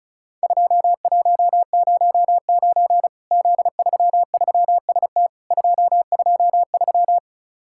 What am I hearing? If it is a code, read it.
Morse code, heard as a keyed tone, about 36 wpm, 700 Hz. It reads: 2109 Z33ST 223
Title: Worked All Europe DX Contest - QTC Exchange - 35wpm